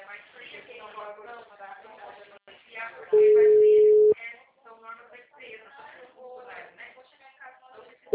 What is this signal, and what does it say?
Ringback tone (Europe, 425 Hz), with speech.